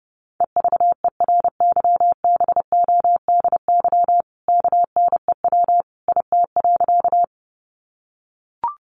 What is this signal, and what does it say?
Morse code, heard as a keyed tone, about 28 wpm, 700 Hz. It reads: EVERYBODY KNEW IT.
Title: Sentences from Top 1400 Words 30wpm